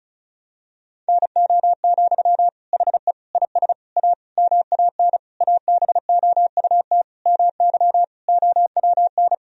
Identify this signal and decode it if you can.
Morse code, heard as a keyed tone, about 36 wpm, 700 Hz. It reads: NO, HE IS A MAN ABOUT MY OWN
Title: Sentences from Top 1000 Words - Review Part 2 - 35wpm